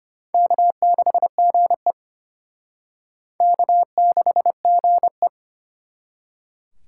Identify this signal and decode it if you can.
Morse code, heard as a keyed tone, about 29 wpm, 700 Hz. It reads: K6GE K6GE
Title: Speed Racing - USA 1x2 Call Signs 25wpm